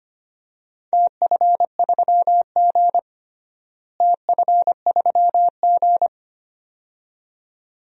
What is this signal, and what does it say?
Morse code, heard as a keyed tone, about 25 wpm, 700 Hz. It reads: TF3G TF3G
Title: DXCC-100 Call Signs - 4 Characters - Repeated 2x - 25wpm